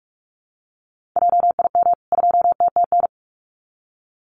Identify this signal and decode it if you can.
Morse code, heard as a keyed tone, about 44 wpm, 700 Hz. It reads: JIM 2TTN
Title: CWT Exchange 45wpm